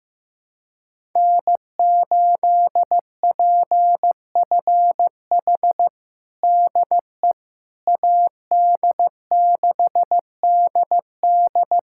Morse code, heard as a keyed tone, about 15 wpm, 700 Hz. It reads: N8PFH DE AD6DD
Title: USA Callsign DE Callsign - 5 Characters - 15wpm